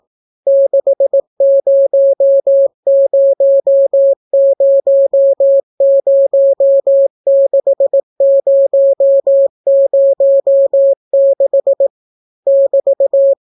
Morse code, heard as a keyed tone, about 18 wpm, 550 Hz. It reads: 600006006 <BT>